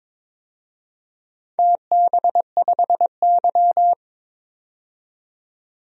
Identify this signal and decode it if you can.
Morse code, heard as a keyed tone, about 22 wpm, 700 Hz. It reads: TB5Y
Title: DXCC-100 Call Signs - 4 Characters - 22wpm